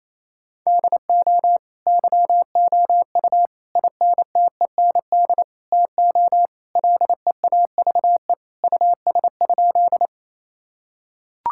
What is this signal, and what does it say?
Morse code, heard as a keyed tone, about 28 wpm, 700 Hz. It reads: DO YOU INTEND TO LEAVE US?
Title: Sentences from Top 1600 Words 28wpm